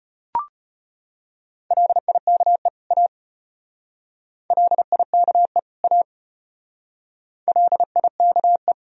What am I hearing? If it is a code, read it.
Morse code, heard as a keyed tone, about 32 wpm, 700 Hz. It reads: LIKE A LIKE A LIKE